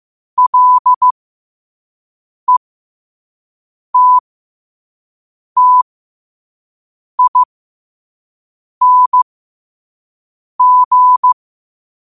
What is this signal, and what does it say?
Morse code, heard as a keyed tone, about 15 wpm, spaced as for 5 wpm, 1000 Hz. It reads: LETTING
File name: 5wpm